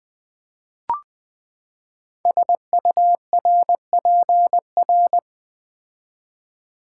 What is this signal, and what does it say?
Morse code, heard as a keyed tone, about 20 wpm, 700 Hz. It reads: SURPR
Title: Build A Word - 5 Letters At A Time - 10-Letter Words - 20wpm